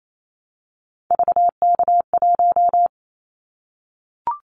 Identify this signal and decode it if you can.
Morse code, heard as a keyed tone, about 28 wpm, 700 Hz. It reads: VK1
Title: Callsign Prefixes for Top 100 DXCC - 28wpm